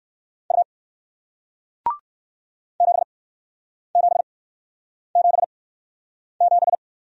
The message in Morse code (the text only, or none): U 7777